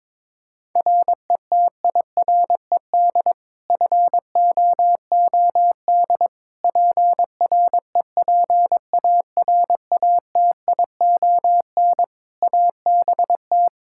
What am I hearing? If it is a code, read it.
Morse code, heard as a keyed tone, about 22 wpm, 700 Hz. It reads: RETIRED FOOD PREPARATION ABT
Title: K1USN SST QSO Practice 22wpm